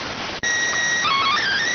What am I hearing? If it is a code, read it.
SSTV (slow-scan television), PD 290.